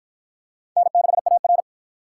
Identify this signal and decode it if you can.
Morse code, heard as a keyed tone, about 46 wpm, 700 Hz. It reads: N6AR